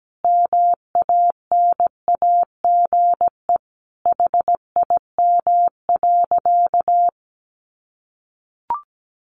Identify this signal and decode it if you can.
Morse code, heard as a keyed tone, about 17 wpm, 700 Hz. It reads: MANAGE HIM.
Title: Sentences from Top 1400 Words 17wpm